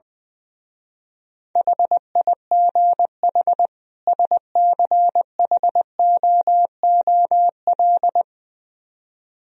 Morse code, heard as a keyed tone, about 20 wpm, 700 Hz. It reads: HIGH SCHOOL
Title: Speed-Racing - Sets of 2 Words 20wpm